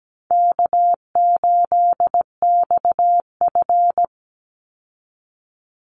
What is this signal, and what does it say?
Morse code, heard as a keyed tone, about 17 wpm, 700 Hz. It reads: K8XF
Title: Easy Call Signs Repeated 2x - 17wpm